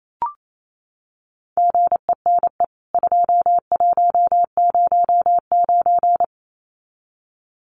Morse code, heard as a keyed tone, about 28 wpm, 700 Hz. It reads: GENE 2109
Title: CWT Exchange 28wpm